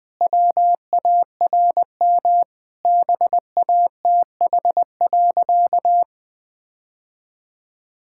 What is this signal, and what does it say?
Morse code, heard as a keyed tone, about 20 wpm, 700 Hz. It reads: WARM BATH.
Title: Sentences from Top 1600 Words 20wpm